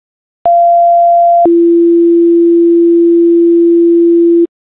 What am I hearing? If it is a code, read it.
Two-tone page, heard as a one-second tone, then a three-second tone, about 690 then 340 Hz, with noise. It is heard from 0.5 to 4.5 s.